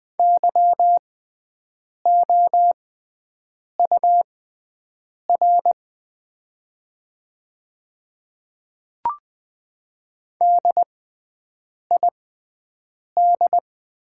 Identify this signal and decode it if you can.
Morse code, heard as a keyed tone, about 20 wpm, spaced as for 7 wpm, 700 Hz. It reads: YOUR DID
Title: 6x Character Spacing - Top 100 Words - 20wpm